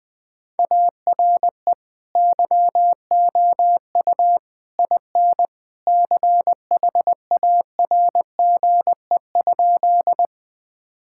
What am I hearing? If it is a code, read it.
Morse code, heard as a keyed tone, about 20 wpm, 700 Hz. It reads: ARE YOU IN CHARGE?
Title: Sentences from Top 1000 Words - Review Part 6 - 20wpm